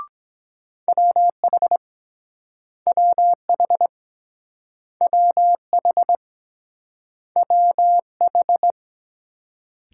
Morse code, heard as a keyed tone, about 24 wpm, 700 Hz. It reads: WH WH WH WH